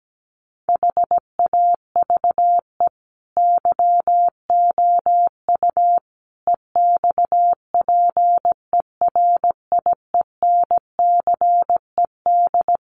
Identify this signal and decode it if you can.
Morse code, heard as a keyed tone, about 17 wpm, 700 Hz. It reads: HAVE YOU EXPERIENCED
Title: Sentences from Top 1100 Words 17wpm